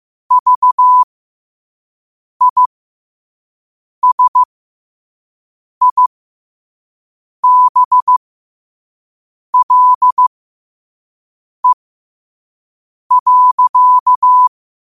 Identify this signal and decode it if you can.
Morse code, heard as a keyed tone, about 15 wpm, spaced as for 5 wpm, 1000 Hz. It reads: VISIBLE.
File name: qotdfiles / 5wpm